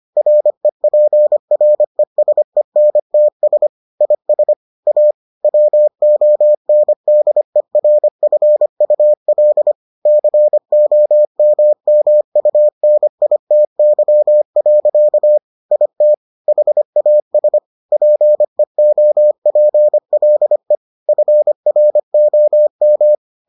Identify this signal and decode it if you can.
Morse code, heard as a keyed tone, about 25 wpm, 600 Hz. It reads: REPRESENTS IS A WONDERFUL COMMUNITY. IT HAS PEOPLE FROM